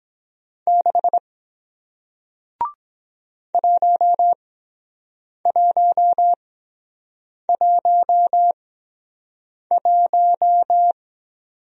Morse code, heard as a keyed tone, about 23 wpm, 700 Hz. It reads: 6 1 1 1 1